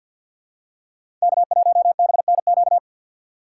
Morse code, heard as a keyed tone, about 52 wpm, 690 Hz. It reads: K1BNQ
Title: USA Call Signs - 6x Character Spacing - 50wpm